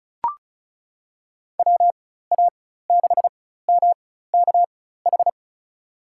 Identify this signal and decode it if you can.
Morse code, heard as a keyed tone, about 36 wpm, spaced as for 16 wpm, 700 Hz. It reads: WA6MKH
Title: USA Call Signs - 4x Character Spacing - 35wpm